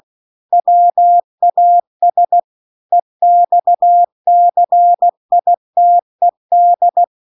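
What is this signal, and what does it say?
Morse code, heard as a keyed tone, about 16 wpm, 700 Hz. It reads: WAS EXCITED
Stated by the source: Morse practice text at 16 WPM.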